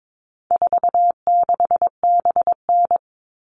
Morse code, heard as a keyed tone, about 22 wpm, 700 Hz. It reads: 46BN
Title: All Cut Numbers - 4-Digit 22wpm